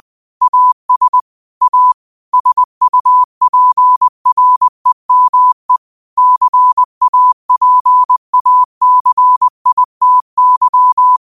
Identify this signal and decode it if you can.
Morse code, heard as a keyed tone, about 20 wpm, 1000 Hz. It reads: AS A SUPREME CAPACITY
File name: qotdfiles / 20wpm